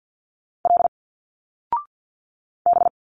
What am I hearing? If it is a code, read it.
Morse code, heard as a keyed tone, about 46 wpm, 700 Hz. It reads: L B
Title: Single Letters 50wpm